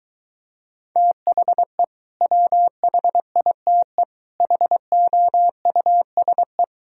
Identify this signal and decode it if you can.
Morse code, heard as a keyed tone, about 23 wpm, 700 Hz. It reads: THE WHITE HOUSE